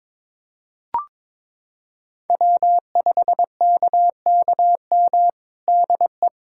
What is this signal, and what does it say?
Morse code, heard as a keyed tone, about 22 wpm, 700 Hz. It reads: W5KKM DE
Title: USA Callsign DE Callsign - 5 Characters - 22wpm